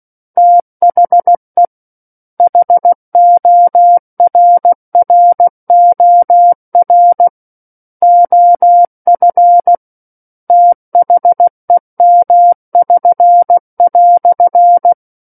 Morse code, heard as a keyed tone, about 16 wpm, 700 Hz. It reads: THE HORROR OF THEM<SN>"
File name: Book One / 07-Chapter 7 - 16 WPM